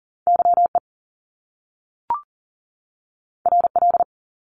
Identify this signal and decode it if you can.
Morse code, heard as a keyed tone, about 39 wpm, 700 Hz. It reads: YE RL
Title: Top Two Letter Combinations in Words 40wpm